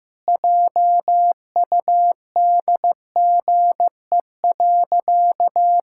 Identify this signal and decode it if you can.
Morse code, heard as a keyed tone, about 15 wpm, 700 Hz. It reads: JUDGE.
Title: Pangram Challenge v2 - 15wpm